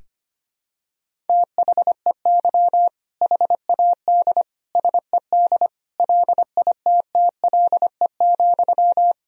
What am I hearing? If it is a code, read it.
Morse code, heard as a keyed tone, about 25 wpm, 700 Hz. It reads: THEY HAD SED LITTLE,
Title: Sentences from Top 400 Words and Abbreviations 25wpm